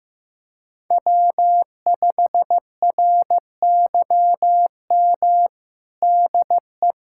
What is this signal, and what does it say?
Morse code, heard as a keyed tone, about 15 wpm, 700 Hz. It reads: W5RYM DE
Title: USA Callsign DE Callsign - 5 Characters - 15wpm